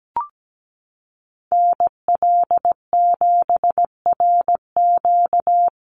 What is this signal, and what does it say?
Morse code, heard as a keyed tone, about 17 wpm, 700 Hz. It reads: NL7RQ
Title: USA Call Signs Repeated 3x 17wpm